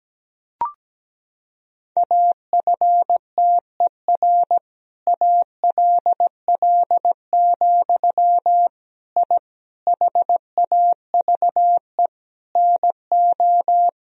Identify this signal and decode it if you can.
Morse code, heard as a keyed tone, about 17 wpm, 700 Hz. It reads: AFTER ALL, I HAVE NO
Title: Sentences from Top 1000 Words - Review Part 3 - 17wpm